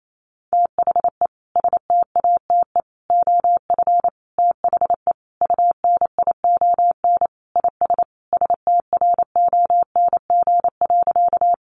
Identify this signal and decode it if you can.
Morse code, heard as a keyed tone, about 28 wpm, 700 Hz. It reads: THE STATE OF THE UNION IS STRONG.
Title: Sentences from Top 1400 Words 28wpm